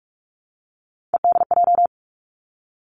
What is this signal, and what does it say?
Morse code, heard as a keyed tone, about 46 wpm, 700 Hz. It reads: EDJ
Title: Three Challenging Letter-Number Combinations 45wpm